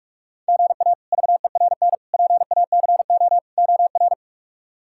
Morse code, heard as a keyed tone, about 46 wpm, 700 Hz. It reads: GA VERN PACO OR
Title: K1USN SST QSO Practice - 45wpm